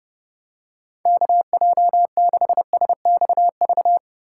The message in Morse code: KJ6SXV

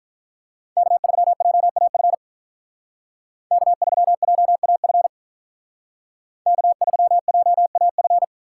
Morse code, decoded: K3JAF K3JAF K3JAF

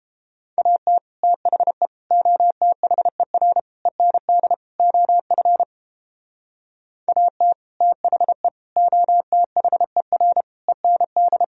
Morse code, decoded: AT THE OTHER END OF AT THE OTHER END

